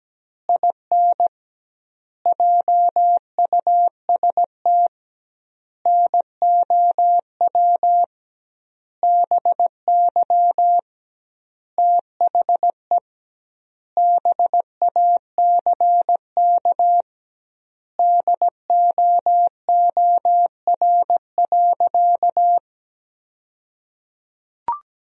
Morse code, heard as a keyed tone, about 17 wpm, 700 Hz. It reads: IN JUST NOW BY THE BACK DOOR.
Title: 2x Word-Spacing - Sentences from Top 300 Words - 17wpm